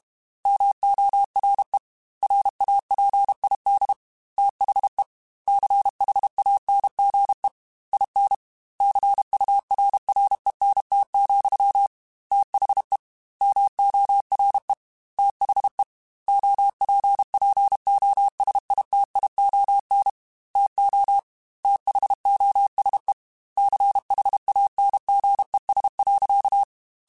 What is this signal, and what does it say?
Morse code, heard as a keyed tone, about 32 wpm, 800 Hz. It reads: MORE RAPID THE CHANGE IN CURRENT, THE MORE THE OPPOSITION TO THOSE CHANGES.
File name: Technical Subjects / Technical Subjects-mp3s / XL @30